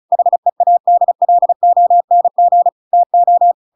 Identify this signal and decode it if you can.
Morse code, heard as a keyed tone, about 34 wpm, 700 Hz. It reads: HEADLONG TO
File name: Pinocchio / Pinocchio ch7 35 35 ik1hge